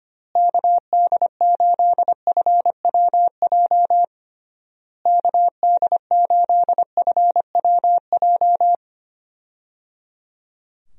Morse code, heard as a keyed tone, about 25 wpm, 700 Hz. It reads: KD8FWJ KD8FWJ
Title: USA Call Signs - 6 Characters - Repeated 3x - 25wpm